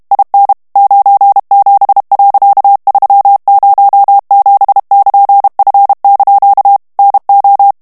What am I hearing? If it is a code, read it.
Morse code, heard as a keyed tone, about 32 wpm, 800 Hz. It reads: IN 97.307<KN>F) NO
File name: morse mp3 / ANGLAIS texte clair 3 vitesse 30 mots minute